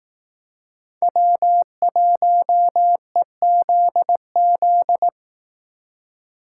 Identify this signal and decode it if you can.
Morse code, heard as a keyed tone, about 18 wpm, 700 Hz. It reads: W1EZZ
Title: Speed Racing - USA 1x3 Call Signs 15wpm